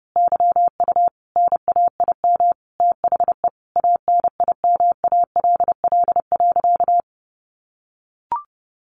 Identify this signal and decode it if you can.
Morse code, heard as a keyed tone, about 30 wpm, 700 Hz. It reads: YU NAIM THE ANIMALL.
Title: A Sound Of Thunder 30wpm